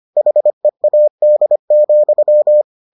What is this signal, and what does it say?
Morse code, heard as a keyed tone, about 25 wpm, 600 Hz. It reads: HEAD,